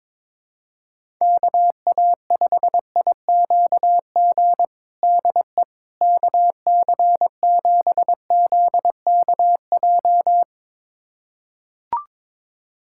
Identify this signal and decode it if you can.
Morse code, heard as a keyed tone, about 21 wpm, 700 Hz. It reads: KA5IQG DE KC7ZKJ
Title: USA Callsign DE Callsign - 6 Characters - 22wpm